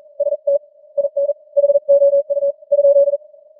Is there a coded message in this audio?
Yes: Morse code from 0.2 to 3.2 s, heard as a keyed tone, about 50 wpm, spaced as for 42 wpm, 590 Hz. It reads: ST IN HOU ?